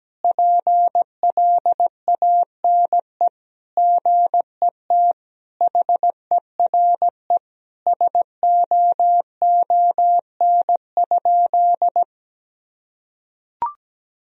Morse code, heard as a keyed tone, about 17 wpm, 700 Hz. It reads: PLANE GET HERE SOON?
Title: Sentences from Top 1000 Words - Review Part 3 - 17wpm